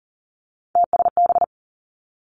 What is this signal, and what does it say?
Morse code, heard as a keyed tone, about 38 wpm, 700 Hz. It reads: TSB
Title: Three Challenging Letter-Number Combinations 40wpm